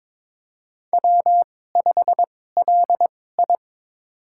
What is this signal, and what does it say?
Morse code, heard as a keyed tone, about 21 wpm, spaced as for 16 wpm, 700 Hz. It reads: W5LI
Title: USA Call Signs - 4 Characters - 2x Character Spacing - 22wpm